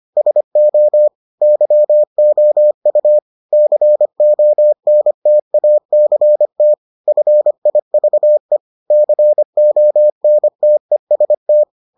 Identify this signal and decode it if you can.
Morse code, heard as a keyed tone, about 25 wpm, 600 Hz. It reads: SO YOU CONTACT FIVE CONTEST